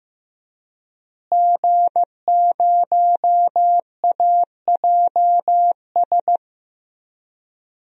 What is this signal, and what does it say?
Morse code, heard as a keyed tone, about 15 wpm, 700 Hz. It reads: G0AJS
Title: DXCC-100 Call Signs - 5 Characters - Repeated 3x - 15wpm